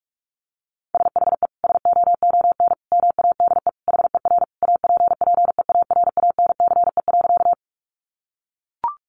Morse code, heard as a keyed tone, about 44 wpm, 700 Hz. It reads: SHE SOON MADE HER APPEARANCE.
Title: Sentences from Top 1100 Words 45wpm